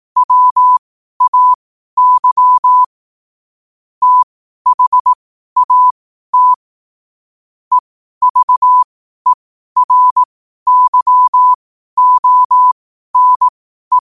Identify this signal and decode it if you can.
Morse code, heard as a keyed tone, about 18 wpm, spaced as for 12 wpm, 1000 Hz. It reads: WAY THAT EVERYONE
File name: 12wpm